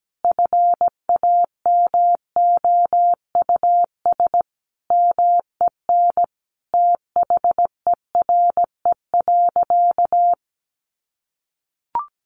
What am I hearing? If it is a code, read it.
Morse code, heard as a keyed tone, about 17 wpm, 700 Hz. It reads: FAMOUS MEN THERE.
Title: Sentences from Top 1500 Words 17wpm